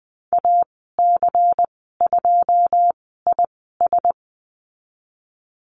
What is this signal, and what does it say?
Morse code, heard as a keyed tone, about 19 wpm, spaced as for 14 wpm, 700 Hz. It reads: AC2IS
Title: 186 Lesson 30g - Letter b - Call Signs 20wpm